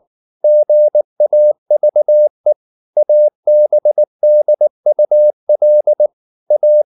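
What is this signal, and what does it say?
Morse code, heard as a keyed tone, about 19 wpm, 600 Hz. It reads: GAVE ABDUL A
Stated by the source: Morse code at 19 wpm from CW med Gnister, reading Ken Follett's novel Never in English.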